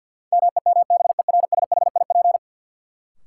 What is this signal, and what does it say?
Morse code, heard as a keyed tone, about 53 wpm, 700 Hz. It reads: MEMBERSHIP
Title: Sentences from Top 1400 Words - New Word Drill - 50wpm